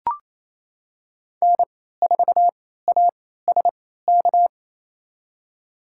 Morse code, heard as a keyed tone, about 27 wpm, spaced as for 16 wpm, 700 Hz. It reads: N4ASK
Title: USA Call Signs - 3x Character Spacing - 28wpm